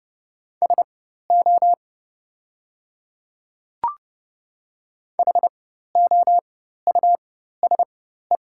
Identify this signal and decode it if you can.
Morse code, heard as a keyed tone, about 32 wpm, spaced as for 14 wpm, 700 Hz. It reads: SO HOUSE
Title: 4x Character Spacing - Top 200 Words - 30wpm